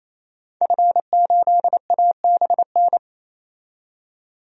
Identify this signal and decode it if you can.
Morse code, heard as a keyed tone, about 28 wpm, 700 Hz. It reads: F8ABN
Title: DXCC-100 Call Signs - 5 Characters - 28wpm